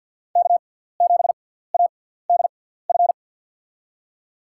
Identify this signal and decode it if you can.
Morse code, heard as a keyed tone, about 49 wpm, spaced as for 17 wpm, 700 Hz. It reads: K7ADF
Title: USA Call Signs - 6x Character Spacing - 50wpm